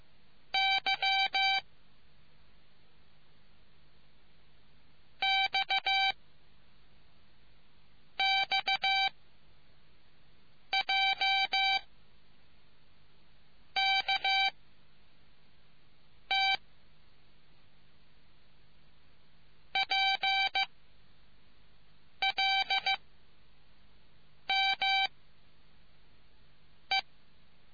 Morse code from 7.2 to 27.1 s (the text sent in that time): X J K T P L M E